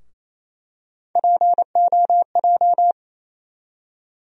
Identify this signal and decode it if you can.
Morse code, heard as a keyed tone, about 27 wpm, 700 Hz. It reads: POJ
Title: USA Call Sign Postfixes - 3 Letters 28wpm